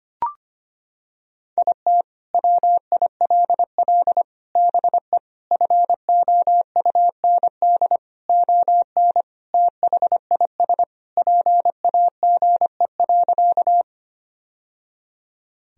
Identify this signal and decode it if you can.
Morse code, heard as a keyed tone, about 25 wpm, 700 Hz. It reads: IT WILL BE FOUND ON THIS PAGE.
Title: Sentences from Top 1000 Words - Review Part 1 - 25wpm